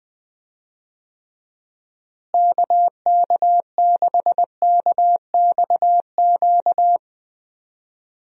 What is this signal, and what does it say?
Morse code, heard as a keyed tone, about 20 wpm, 700 Hz. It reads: KK6KXQ
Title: USA Call Signs - 6 Characters - Repeated 3x - 20wpm